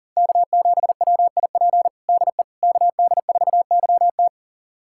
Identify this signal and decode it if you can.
Morse code, heard as a keyed tone, about 40 wpm, 700 Hz. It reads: K7WIP DE KD4YT
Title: USA Callsign DE Callsign - 5 Characters - 40wpm